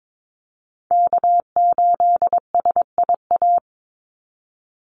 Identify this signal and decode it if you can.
Morse code, heard as a keyed tone, about 22 wpm, 700 Hz. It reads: K8SIA
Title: Call Signs Repeated 3x - 22wpm